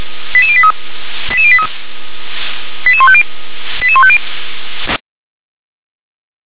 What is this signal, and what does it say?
Five-tone selective call (ZVEI), heard as quick runs of five tones, tones of about 70 ms. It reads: 88083 88083 88150 88150